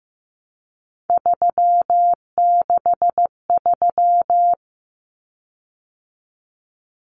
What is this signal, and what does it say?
Morse code, heard as a keyed tone, about 15 wpm, 700 Hz. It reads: 363
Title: Common Cut Numbers - 3-Digit 15wpm